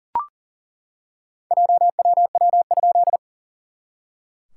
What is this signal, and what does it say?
Morse code, heard as a keyed tone, about 40 wpm, 690 Hz. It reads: JWW?
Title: SOTA QSO Practice 40wpm